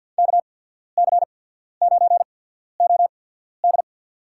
Morse code, heard as a keyed tone, about 53 wpm, spaced as for 14 wpm, 700 Hz. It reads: KC9OD
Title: USA Call Signs - 5 Characters - 8x Character Spacing - 50wpm